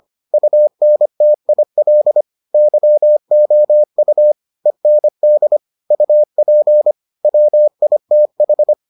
Morse code, heard as a keyed tone, about 25 wpm, 600 Hz. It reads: UNTIL YOU END UP WITH